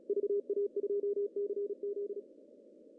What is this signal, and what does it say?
Morse code, heard as a keyed tone, about 35 wpm, 400 Hz. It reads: VA2CZ
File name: CW / cqww-cw-k5tr